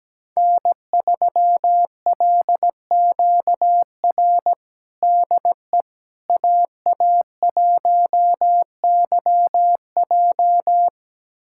Morse code, heard as a keyed tone, about 17 wpm, 700 Hz. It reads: N3LQR DE AA1YJ
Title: USA Callsign DE Callsign - 5 Characters - 17wpm